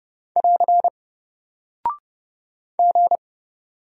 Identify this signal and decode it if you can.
Morse code, heard as a keyed tone, about 30 wpm, 700 Hz. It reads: <AR> G
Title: Single Character (Letter:Number:Punctuation:Prosign) Alt 30wpm